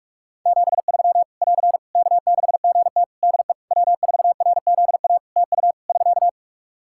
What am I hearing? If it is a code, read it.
Morse code, heard as a keyed tone, about 45 wpm, 700 Hz. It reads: 73 <AR> K6GT DE W4RZA TU <SK>
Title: Practice QSO - Exchange 4 - Ending - 45wpm